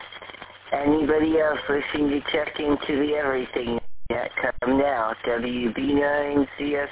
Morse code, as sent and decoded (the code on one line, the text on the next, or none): none